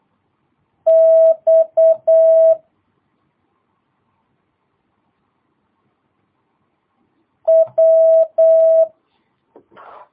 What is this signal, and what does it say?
Morse code, heard as a keyed tone, about 8 wpm, 650 Hz. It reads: X W
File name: jh8cbh / 2022cw